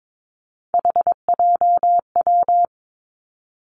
Morse code, heard as a keyed tone, about 22 wpm, 700 Hz. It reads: HJW
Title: Three Challenging Letter-Number Combinations 22wpm